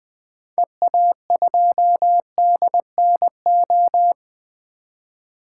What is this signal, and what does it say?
Morse code, heard as a keyed tone, about 20 wpm, 700 Hz. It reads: EA2DNO
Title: 2023 SOTA Activator Call Signs - 20wpm